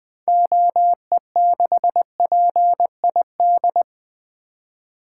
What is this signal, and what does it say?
Morse code, heard as a keyed tone, about 20 wpm, 700 Hz. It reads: OE6PID
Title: 2023 SOTA Activator Call Signs - 20wpm